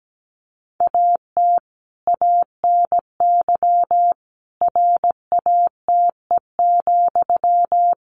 Morse code, heard as a keyed tone, about 17 wpm, 700 Hz. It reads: AT ANY RATE,
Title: Sentences from Top 1500 Words 17wpm